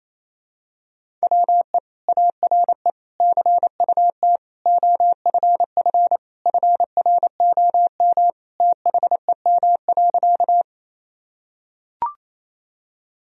Morse code, heard as a keyed tone, about 28 wpm, 700 Hz. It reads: WE ARE CUT OFF FROM THEM.
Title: Sentences from Top 1000 Words - Review Part 1 - 28wpm